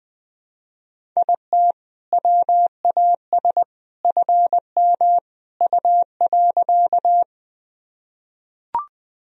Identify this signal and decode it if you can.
Morse code, heard as a keyed tone, about 20 wpm, 700 Hz. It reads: IT WAS FM U.
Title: Sentences from Top 100 Words and Abbreviations 20wpm